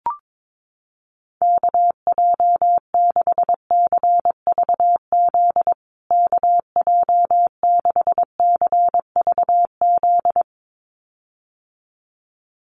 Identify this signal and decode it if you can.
Morse code, heard as a keyed tone, about 22 wpm, 700 Hz. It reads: KJ6CVZ KJ6CVZ
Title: USA Call Signs Repeated 2x 22wpm